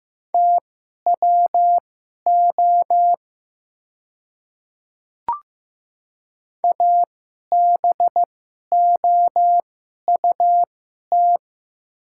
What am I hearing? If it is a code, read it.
Morse code, heard as a keyed tone, about 15 wpm, spaced as for 11 wpm, 700 Hz. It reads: TWO ABOUT
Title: 2x Character Spacing - Top 100 Words - 15wpm